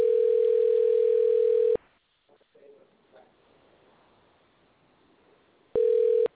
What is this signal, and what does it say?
Ringback tone (North America), 440 + 480 Hz.